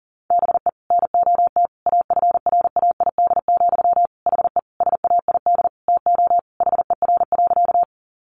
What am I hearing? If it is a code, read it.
Morse code, heard as a keyed tone, about 40 wpm, 700 Hz. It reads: BE NOT AFRAID, HE SAID TO HER.
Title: Sentences from Top 1500 Words 40wpm